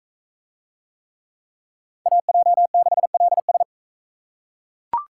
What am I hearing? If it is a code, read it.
Morse code, heard as a keyed tone, about 41 wpm, 700 Hz. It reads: AJ6LS